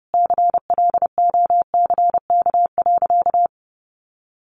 Morse code, heard as a keyed tone, about 30 wpm, 700 Hz. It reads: CLOCK.
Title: Easy Phrases 30wpm